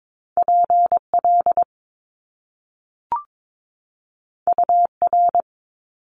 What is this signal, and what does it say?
Morse code, heard as a keyed tone, about 22 wpm, 700 Hz. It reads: PL UR
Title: Top Two Letter Combinations in Words 22wpm